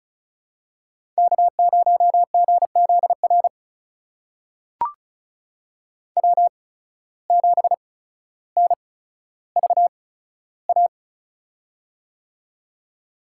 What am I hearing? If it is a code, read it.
Morse code, heard as a keyed tone, about 36 wpm, spaced as for 15 wpm, 700 Hz. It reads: K0GZR W7NVA
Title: USA Call Signs - 5 Characters - 8x Character Spacing - 35wpm